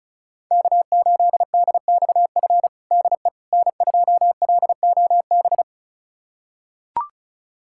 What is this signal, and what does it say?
Morse code, heard as a keyed tone, about 36 wpm, 700 Hz. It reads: K8DXF DE N2LOB
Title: USA Callsign DE Callsign - 5 Characters - 35wpm